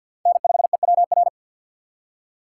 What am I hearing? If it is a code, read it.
Morse code, heard as a keyed tone, about 61 wpm, 700 Hz. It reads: N5EWR
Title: Speed Racing - USA 1x3 Call Signs 50wpm